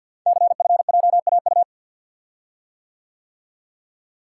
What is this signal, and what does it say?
Morse code, heard as a keyed tone, about 48 wpm, 690 Hz. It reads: CFJAU
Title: Code Groups - 5 Letters - 50wpm